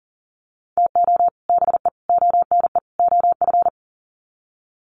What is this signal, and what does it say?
Morse code, heard as a keyed tone, about 40 wpm, 700 Hz. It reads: TO BE ONE OF
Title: Sets of 4 Words 40wpm